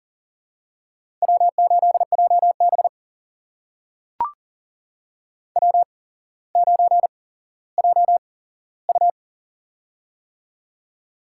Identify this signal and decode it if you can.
Morse code, heard as a keyed tone, about 40 wpm, spaced as for 17 wpm, 700 Hz. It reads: W8JB W9JU
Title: USA Call Signs - 4 Characters - 8x Character Spacing - 40wpm